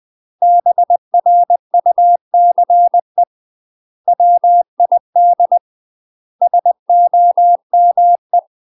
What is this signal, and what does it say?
Morse code, heard as a keyed tone, about 20 wpm, 700 Hz. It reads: BRUCE WID SOME
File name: UK MASTER QSOs TRAINING / 20WPM / UK MASTER QSOs 0053